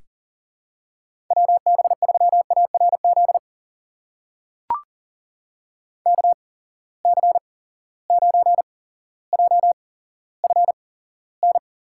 Morse code, heard as a keyed tone, about 40 wpm, spaced as for 17 wpm, 700 Hz. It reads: WB3ARZ KC9JFN
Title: USA Call Signs - 6 Characters - 8x Character Spacing - 40wpm